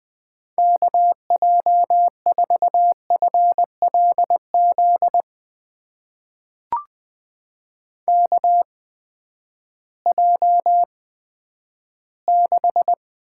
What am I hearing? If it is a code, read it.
Morse code, heard as a keyed tone, about 20 wpm, spaced as for 11 wpm, 700 Hz. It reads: KJ4FLZ KJ6
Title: USA Call Signs - 6 Characters - 8x Character Spacing - 20wpm